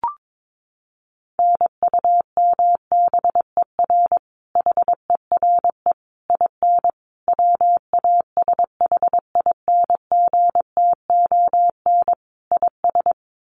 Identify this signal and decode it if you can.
Morse code, heard as a keyed tone, about 22 wpm, 700 Hz. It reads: NUMBER HERE IN WASHINGTON IS
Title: Sets of 5 Words Encore2 22wpm